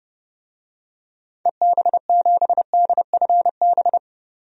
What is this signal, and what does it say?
Morse code, heard as a keyed tone, about 32 wpm, 700 Hz. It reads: EB7DFB
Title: DXCC-100 Call Signs - 6 Characters - 30wpm